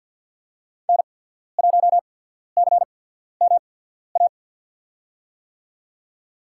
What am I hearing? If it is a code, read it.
Morse code, heard as a keyed tone, about 54 wpm, spaced as for 14 wpm, 700 Hz. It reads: N1CMA
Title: USA Call Signs - 5 Characters - 8x Character Spacing - 50wpm